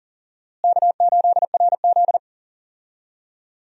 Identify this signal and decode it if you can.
Morse code, heard as a keyed tone, about 40 wpm, 700 Hz. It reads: K8RZ
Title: USA Call Signs - 4 Characters - Repeated 2x - 40wpm